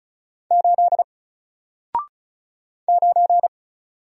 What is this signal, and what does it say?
Morse code, heard as a keyed tone, about 36 wpm, 700 Hz. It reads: 8 9
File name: Single-Letter-Number-35wpm